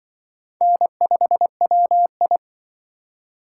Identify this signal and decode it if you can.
Morse code, heard as a keyed tone, about 24 wpm, 700 Hz. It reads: N5WI
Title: Speed Racing - USA 1x2 Call Signs 20wpm